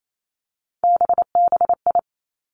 Morse code, heard as a keyed tone, about 28 wpm, 690 Hz. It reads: BBI
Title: Three Challenging Letter-Number Combinations 28wpm